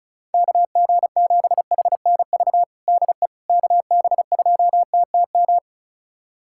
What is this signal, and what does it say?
Morse code, heard as a keyed tone, about 34 wpm, 700 Hz. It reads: KG7HNV DE KB2TTM
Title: USA Callsign DE Callsign - 6 Characters - 35wpm